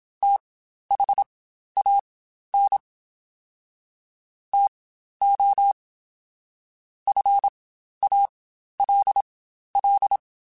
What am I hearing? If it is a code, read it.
Morse code, heard as a keyed tone, about 27 wpm, spaced as for 12 wpm, 800 Hz. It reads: THAN TO FALL
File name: Quotes Spaced / Quotes Spaced-mp3s / StandFor sp @25